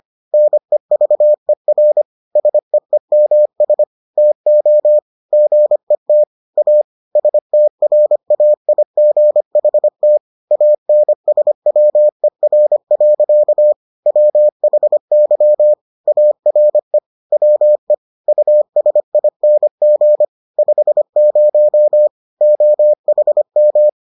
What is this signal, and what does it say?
Morse code, heard as a keyed tone, about 25 wpm, 600 Hz. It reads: NEVER SEEMS TO GET A STRAIGHT ANSWER. WHY ARE WE USING 50 OHM